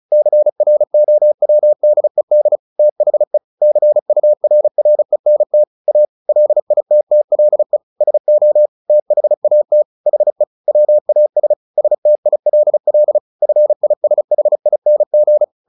Morse code, heard as a keyed tone, about 34 wpm, 600 Hz. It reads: CROWDED THE CURRENT A LITTLE SO THAT HE WAS STILL FISHING
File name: the old man and the sea / the old man and the sea 1 35wpm